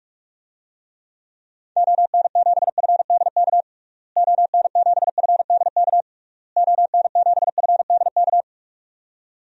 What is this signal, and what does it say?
Morse code, heard as a keyed tone, about 46 wpm, 700 Hz. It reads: ON7FDK ON7FDK ON7FDK
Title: DXCC-100 Call Signs - 6 Characters - Repeated 3x - 45wpm